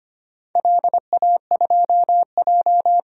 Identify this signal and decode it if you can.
Morse code, heard as a keyed tone, about 25 wpm, 700 Hz. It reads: LA2J
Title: DXCC-100 Call Signs - 4 Characters - Repeated 3x - 25wpm